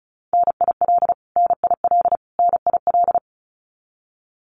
Morse code, heard as a keyed tone, about 34 wpm, 700 Hz. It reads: NIL NIL NIL
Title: Abbreviations Repeated 3x - 35wpm